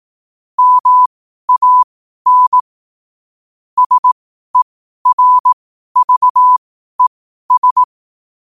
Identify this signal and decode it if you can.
Morse code, heard as a keyed tone, about 18 wpm, spaced as for 12 wpm, 1000 Hz. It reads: MAN SERVES
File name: qotdfiles / 12wpm